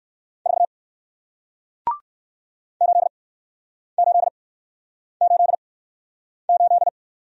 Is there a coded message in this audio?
Yes: Morse code from 0.5 to 6.9 s, heard as a keyed tone, about 60 wpm, spaced as for 9 wpm, 700 Hz. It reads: 4 8888